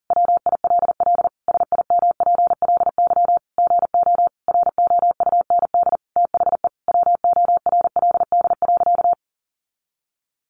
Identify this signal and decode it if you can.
Morse code, heard as a keyed tone, about 40 wpm, 700 Hz. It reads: WILL SIMPLY GO ROUND THE WORLD.
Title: Sentences from Top 600 Words 40wpm